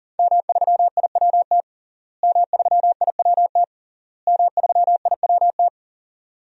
Morse code, heard as a keyed tone, about 37 wpm, 700 Hz. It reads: M3IWT M3IWT M3IWT
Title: DXCC-100 Call Signs - 5 Characters - Repeated 3x - 40wpm